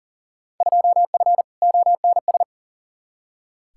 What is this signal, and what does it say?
Morse code, heard as a keyed tone, about 40 wpm, 700 Hz. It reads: 2F ONS
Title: Field Day Exchanges - 40wpm